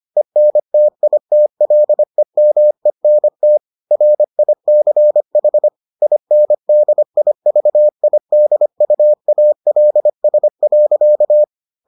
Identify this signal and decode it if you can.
Morse code, heard as a keyed tone, about 25 wpm, 600 Hz. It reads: ENTITLEMENT RICH INDIVIDUALS.